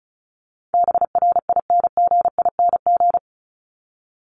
Morse code, heard as a keyed tone, about 34 wpm, 700 Hz. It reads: BRINGING
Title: ING Suffix - Top 100 Words with Base Words 35wpm